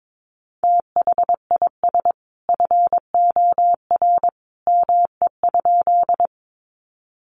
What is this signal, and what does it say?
Morse code, heard as a keyed tone, about 22 wpm, 700 Hz. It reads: THIS FOR ME?
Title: Sentences from Top 200 Words 22wpm